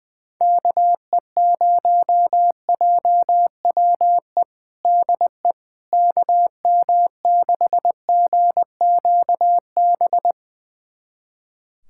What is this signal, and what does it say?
Morse code, heard as a keyed tone, about 20 wpm, 700 Hz. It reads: KE0JWE DE KM6GQB
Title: USA Callsign DE Callsign 20wpm